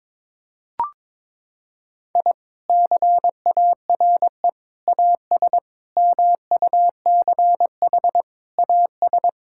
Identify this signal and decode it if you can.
Morse code, heard as a keyed tone, about 22 wpm, 700 Hz. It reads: I CARE AS MUCH AS
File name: practice / 22wpm / Sentences with 3-Word Phrases - 22wpm